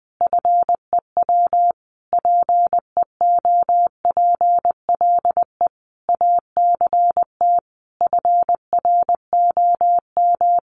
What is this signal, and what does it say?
Morse code, heard as a keyed tone, about 20 wpm, 700 Hz. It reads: FEW PEOPLE ACT FROM
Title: Sentences from Top 1400 Words 20wpm